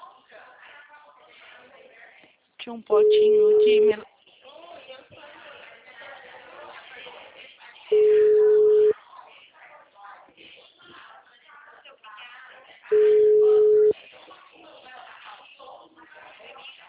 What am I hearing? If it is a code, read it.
Ringback tone (Europe, 425 Hz), with speech.
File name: out- -2014- - - .2713